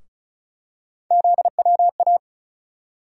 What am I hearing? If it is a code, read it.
Morse code, heard as a keyed tone, about 36 wpm, 700 Hz. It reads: ZWA
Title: USA Call Sign Postfixes - 3 Letters 35wpm